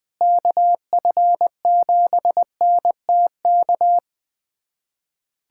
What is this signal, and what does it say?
Morse code, heard as a keyed tone, about 20 wpm, 700 Hz. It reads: KF7NTK
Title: USA Callsign DE Callsign - 6 Characters - 20wpm